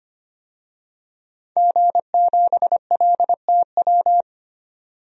Morse code, heard as a keyed tone, about 25 wpm, 700 Hz. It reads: G7LTW
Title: DXCC-100 Call Signs - 5 Characters - 25wpm